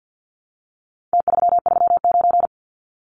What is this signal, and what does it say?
Morse code, heard as a keyed tone, about 50 wpm, 700 Hz. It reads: T339
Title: Numbers - Contest Style - 4-Digit 50wpm